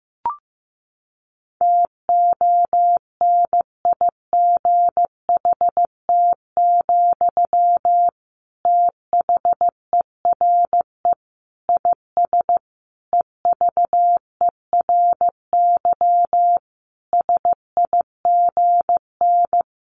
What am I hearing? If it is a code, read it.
Morse code, heard as a keyed tone, about 15 wpm, 700 Hz. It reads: TONIGHT, THERE IS EVERY SIGN
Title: Sentences from Top 1400 Words 15wpm